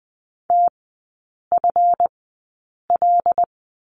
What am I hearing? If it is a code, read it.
Morse code, heard as a keyed tone, about 20 wpm, spaced as for 8 wpm, 700 Hz. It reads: TFL
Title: 166 Lesson 27e - Number 7 - Three Characters 20wpm